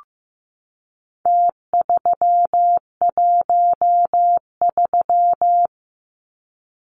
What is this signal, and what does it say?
Morse code, heard as a keyed tone, about 15 wpm, 700 Hz. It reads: T313
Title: Numbers - Contest Style - 4-Digit 15wpm